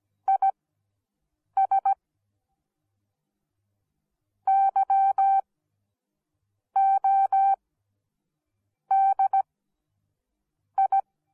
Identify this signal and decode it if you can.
Morse code, heard as a keyed tone, about 17 wpm, spaced as for 6 wpm, 780 Hz. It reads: IS YODI